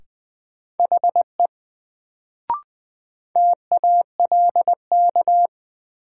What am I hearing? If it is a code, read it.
Morse code, heard as a keyed tone, about 20 wpm, 700 Hz. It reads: HE TALK
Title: Sets of 2 Words - New Word Drill - 20wpm